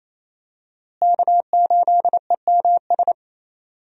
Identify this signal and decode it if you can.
Morse code, heard as a keyed tone, about 27 wpm, 700 Hz. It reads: K8EMS
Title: USA Call Signs - 5 Characters - Repeated 3x - 28wpm